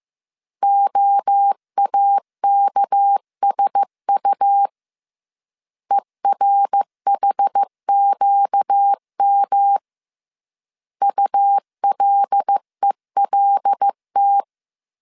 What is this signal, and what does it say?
Morse code, heard as a keyed tone, about 15 wpm, 800 Hz. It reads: OAKSU ERHQM ULELT
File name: CW / DET / 1 6 1500C S A 14wpm 2921